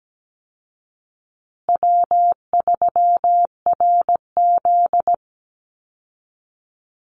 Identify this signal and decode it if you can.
Morse code, heard as a keyed tone, about 17 wpm, 700 Hz. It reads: W3RZ
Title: Easy Call Signs Repeated 2x - 17wpm